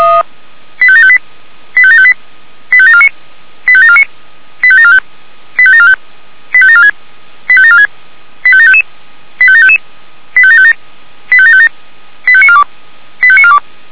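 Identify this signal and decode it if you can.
Five-tone selective call (ZVEI), heard as quick runs of five tones, tones of about 70 ms. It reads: 85758 85758 85749 85749 85745 85745 85746 85746 85755 85755 85757 85757 85942 85942
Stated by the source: Received by SDR as a ZVEI five-tone selective call.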